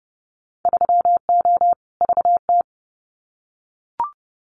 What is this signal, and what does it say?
Morse code, heard as a keyed tone, about 30 wpm, 700 Hz. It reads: 3O VT
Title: Winter Field Day Exchanges Encore 30wpm